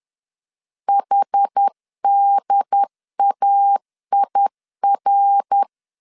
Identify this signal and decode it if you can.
Morse code, heard as a keyed tone, about 10 wpm, 800 Hz. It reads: HDAIR
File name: CW / DET / 1 3 1500C S A 10wpm 3447